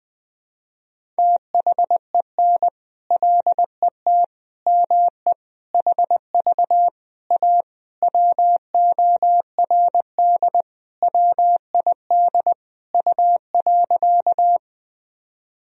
Morse code, heard as a keyed tone, about 21 wpm, 700 Hz. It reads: THEN LET ME HV A WORD WID U.
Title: Sentences from Top 300 Words and Abbreviations 20wpm